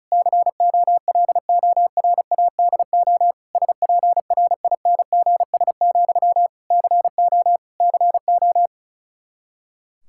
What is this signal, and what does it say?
Morse code, heard as a keyed tone, about 34 wpm, 700 Hz. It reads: COLORADO SPRINGS, CO CO
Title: K1USN SST QSO Practice 35wpm